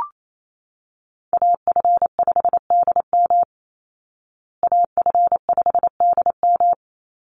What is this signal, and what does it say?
Morse code, heard as a keyed tone, about 28 wpm, 700 Hz. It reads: AF5DM AF5DM
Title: Call Signs Repeated 2x - 28wpm